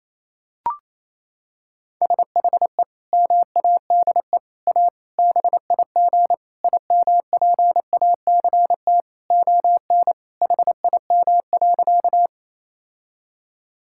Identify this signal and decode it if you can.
Morse code, heard as a keyed tone, about 28 wpm, 700 Hz. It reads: SHE MADE A BIG IMPACT ON HIM.
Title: Sentences from Top 1000 Words - Review Part 6 - 28wpm